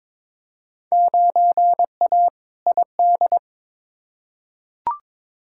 Morse code, heard as a keyed tone, about 22 wpm, 700 Hz. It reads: 9A ID
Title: Field Day Exchanges Encore - 22wpm